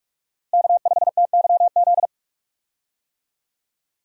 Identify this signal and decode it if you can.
Morse code, heard as a keyed tone, about 44 wpm, 700 Hz. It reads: K5TYZ
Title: USA Call Signs - 5 Characters - Repeated 3x - 45wpm